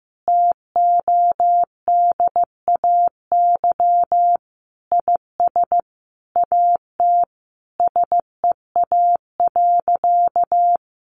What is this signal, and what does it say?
Morse code, heard as a keyed tone, about 15 wpm, 700 Hz. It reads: TODAY IS AT SEA.
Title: Sentences from Top 1400 Words 15wpm